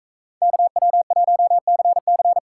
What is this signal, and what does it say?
Morse code, heard as a keyed tone, about 42 wpm, 700 Hz. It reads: KW1CC